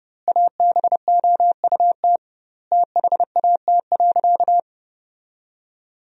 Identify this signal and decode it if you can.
Morse code, heard as a keyed tone, about 32 wpm, 700 Hz. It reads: ABOUT THAT.
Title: 2x Word-Spacing - Sentences from Top 400 Words - 30wpm